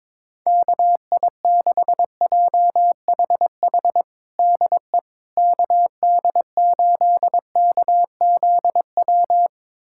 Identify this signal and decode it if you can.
Morse code, heard as a keyed tone, about 22 wpm, 700 Hz. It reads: KI6JHH DE KD8KZW
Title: USA Callsign DE Callsign 22wpm